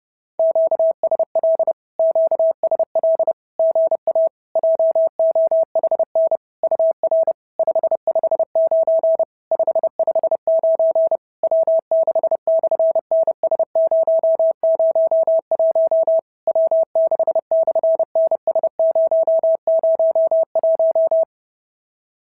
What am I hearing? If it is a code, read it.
Morse code, heard as a keyed tone, about 30 wpm, 650 Hz. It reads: QSL QSL GA JOHN UR 559 559 W6/NS001 W6/NS001
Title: SOTA QSO Practice 30wpm